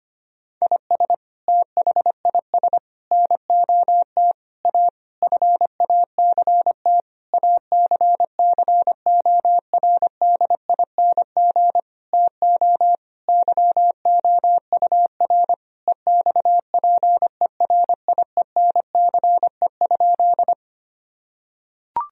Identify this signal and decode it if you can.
Morse code, heard as a keyed tone, about 25 wpm, 700 Hz. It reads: IS THIS NOT A FACT ACCORDING TO YOUR EXPERIENCE?
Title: Sentences from Top 1000 Words - Review Part 4 - 25wpm